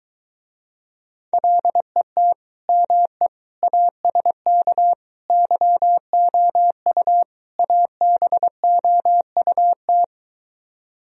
Morse code, heard as a keyed tone, about 23 wpm, 700 Hz. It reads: LET ME ASK YOU ABOUT